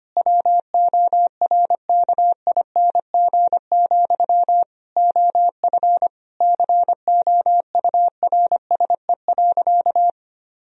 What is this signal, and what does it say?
Morse code, heard as a keyed tone, about 25 wpm, 700 Hz. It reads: WORKING, OF COURSE.
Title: Sentences from Top 1000 Words - Review Part 4 - 25wpm